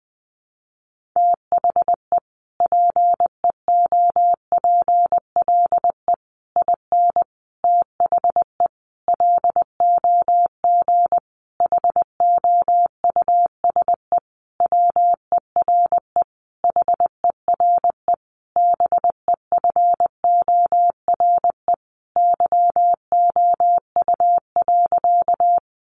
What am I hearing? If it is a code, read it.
Morse code, heard as a keyed tone, about 20 wpm, 700 Hz. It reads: THE PEOPLE IN THE LOG HOUSE WERE HERE BEFORE YOU.
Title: Sentences from Top 1400 Words 20wpm